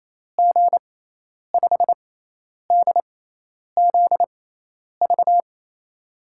Morse code, heard as a keyed tone, about 28 wpm, spaced as for 10 wpm, 700 Hz. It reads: G5DZV
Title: USA Call Signs - 6 Characters - 6x Character Spacing - 28wpm